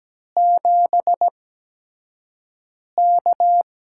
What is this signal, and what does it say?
Morse code, heard as a keyed tone, about 17 wpm, spaced as for 5 wpm, 700 Hz. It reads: 7K
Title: USA Call Signs - 4 Characters - 8x Character Spacing - 17wpm